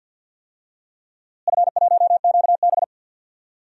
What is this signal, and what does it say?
Morse code, heard as a keyed tone, about 46 wpm, 700 Hz. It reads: F1QB
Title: DXCC-100 Call Signs - 4 Characters - Repeated 2x - 50wpm